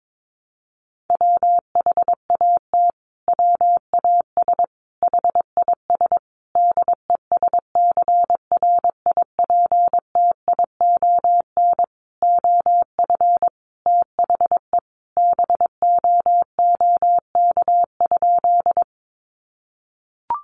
Morse code, heard as a keyed tone, about 22 wpm, 700 Hz. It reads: WHAT WAS HIS DESCRIPTION OF THE BOOK?
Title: Sentences from Top 1300 Words 22wpm